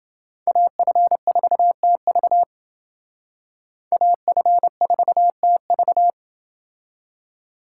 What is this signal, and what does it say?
Morse code, heard as a keyed tone, about 28 wpm, 700 Hz. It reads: AF4TV AF4TV